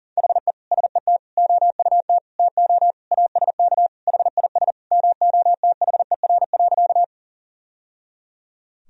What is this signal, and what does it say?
Morse code, heard as a keyed tone, about 40 wpm, 700 Hz. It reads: HE SET OUT TO ASK HIS MOTHER.
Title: Sentences from Top 1000 Words - Review Part 1 - 40wpm